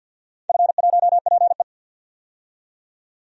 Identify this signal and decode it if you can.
Morse code, heard as a keyed tone, about 47 wpm, 700 Hz. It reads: F1PE
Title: DXCC-100 Call Signs - 4 Characters - 50wpm